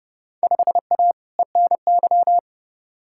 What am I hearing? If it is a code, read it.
Morse code, heard as a keyed tone, about 30 wpm, 700 Hz. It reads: 5A ENY
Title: Field Day Exchanges - 30wpm